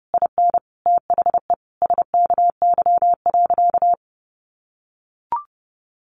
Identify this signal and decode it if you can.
Morse code, heard as a keyed tone, about 30 wpm, 700 Hz. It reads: IN THE SKY.
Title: Sentences from Top 1200 Words 30wpm